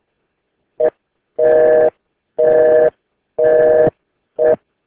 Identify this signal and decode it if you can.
Busy tone (North America), 480 + 620 Hz.